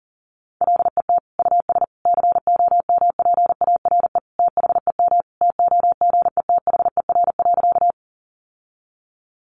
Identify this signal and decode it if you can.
Morse code, heard as a keyed tone, about 37 wpm, 700 Hz. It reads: LET US COMPARE THEM TOGETHER.
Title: Sentences from Top 1300 Words 40wpm